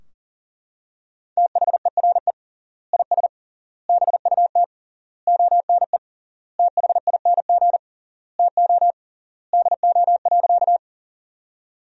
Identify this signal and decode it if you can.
Morse code, heard as a keyed tone, about 40 wpm, 700 Hz. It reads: THERE IS BUT ONE THING TO DO.
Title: 3x Word-Spacing - Sentences from Top 100 Words - 40wpm